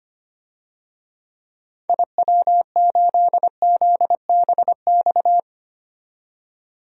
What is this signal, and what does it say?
Morse code, heard as a keyed tone, about 25 wpm, 700 Hz. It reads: IW8ZBX
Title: DXCC-100 Call Signs - 6 Characters - 25wpm